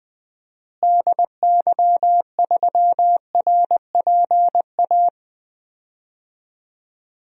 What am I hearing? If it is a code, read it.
Morse code, heard as a keyed tone, about 20 wpm, 700 Hz. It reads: DY3RPA
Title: DXCC-100 Call Signs - 6 Characters - Repeated 2x - 20wpm